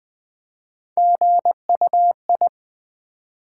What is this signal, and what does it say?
Morse code, heard as a keyed tone, about 20 wpm, 700 Hz. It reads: GUI
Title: Build A Word - 3 Letters At A Time - 9-Letter Words - 20wpm